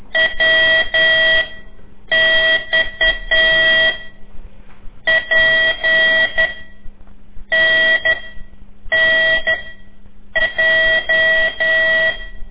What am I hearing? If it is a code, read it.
Morse code, heard as a keyed tone, about 10 wpm, spaced as for 7 wpm, 1950 Hz, with noise. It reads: WX P N N J
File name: CORSI CW